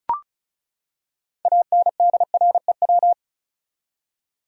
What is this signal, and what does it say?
Morse code, heard as a keyed tone, about 36 wpm, 700 Hz. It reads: ANDREW
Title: USA Common Names - 35wpm